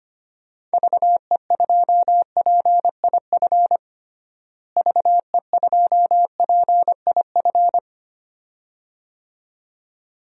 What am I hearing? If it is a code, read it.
Morse code, heard as a keyed tone, about 25 wpm, 700 Hz. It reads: VE2PIF VE2PIF
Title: DXCC-100 Call Signs - 6 Characters - Repeated 2x - 25wpm